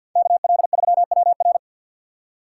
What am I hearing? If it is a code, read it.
Morse code, heard as a keyed tone, about 53 wpm, 700 Hz. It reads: KL3WR
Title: USA Call Signs - 5 Characters - 6x Character Spacing - 50wpm